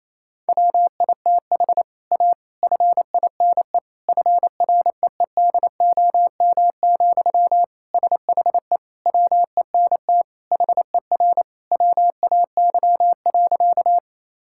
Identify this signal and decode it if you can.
Morse code, heard as a keyed tone, about 28 wpm, 700 Hz. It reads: WITH A FINE FREEDOM, SHE WENT HER WAY.
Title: Sentences from Top 1000 Words - Review Part 5 - 28wpm